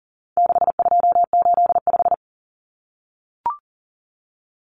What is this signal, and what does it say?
Morse code, heard as a keyed tone, about 40 wpm, 700 Hz. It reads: 6285
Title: Common Cut Numbers - 4-Digit 40wpm